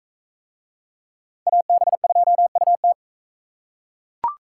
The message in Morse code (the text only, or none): AB2UT